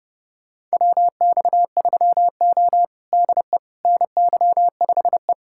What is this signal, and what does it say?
Morse code, heard as a keyed tone, about 32 wpm, 700 Hz. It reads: WX3O DE NY5E
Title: USA Callsign DE Callsign - 4 Characters - 30wpm